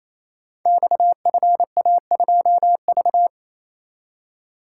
Morse code, heard as a keyed tone, about 27 wpm, 700 Hz. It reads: XFA2V
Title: Code Groups - 5 Letter:Numbers - 28wpm